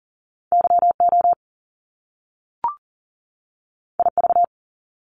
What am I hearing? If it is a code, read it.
Morse code, heard as a keyed tone, about 40 wpm, 700 Hz. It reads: YO IV
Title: Top Two Letter Combinations in Words 40wpm